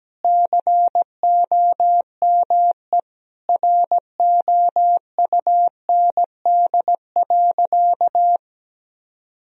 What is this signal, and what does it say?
Morse code, heard as a keyed tone, about 17 wpm, 700 Hz. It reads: COME ROUND.
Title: Sentences from Top 1000 Words - Review Part 1 - 17wpm